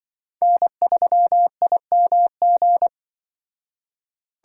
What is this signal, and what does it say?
Morse code, heard as a keyed tone, about 25 wpm, 700 Hz. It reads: N3IMG
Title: Speed Racing - USA 1x3 Call Signs 20wpm